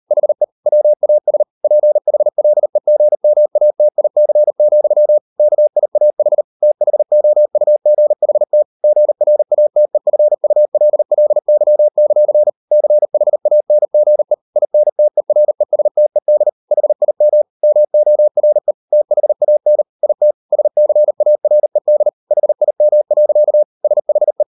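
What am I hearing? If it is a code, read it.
Morse code, heard as a keyed tone, about 40 wpm, 600 Hz. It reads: HE WAS PHLEGMATIC, KIAH THOUGHT GRATEFULLY; CHANGE INTERESTED HIM MORE THAN IT SCARED HIM. SHE
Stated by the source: Morse code practice at 38 WPM, sending Ken Follett's novel Never in English, from CW med Gnister.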